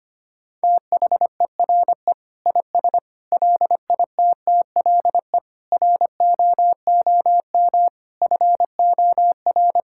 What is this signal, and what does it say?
Morse code, heard as a keyed tone, about 25 wpm, 700 Hz. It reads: THERE IS LITTLE ROOM FOR
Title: Sentences from Top 1000 Words - Review Part 6 - 25wpm